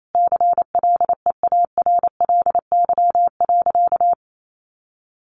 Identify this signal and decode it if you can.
Morse code, heard as a keyed tone, about 28 wpm, 700 Hz. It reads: CLEARLY.
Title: Sentences from Top 1300 Words 28wpm